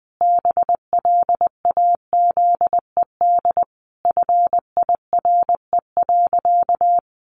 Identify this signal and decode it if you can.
Morse code, heard as a keyed tone, about 20 wpm, 700 Hz. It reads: BLAZED FIRE.
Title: A Sound Of Thunder 20wpm